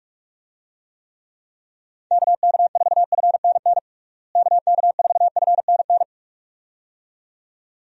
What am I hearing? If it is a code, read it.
Morse code, heard as a keyed tone, about 46 wpm, 700 Hz. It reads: KK4FNN KK4FNN
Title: USA Call Signs - 6 Characters - Repeated 2x - 45wpm